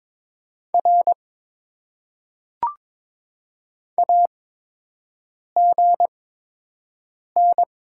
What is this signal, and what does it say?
Morse code, heard as a keyed tone, about 22 wpm, spaced as for 6 wpm, 700 Hz. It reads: R AGN
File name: Top 90 QSO Elements - 8x Character Spacing - 22wpm